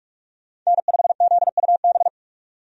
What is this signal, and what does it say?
Morse code, heard as a keyed tone, about 46 wpm, 700 Hz. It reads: N5ZUB
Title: USA Call Signs - 5 Characters - 45wpm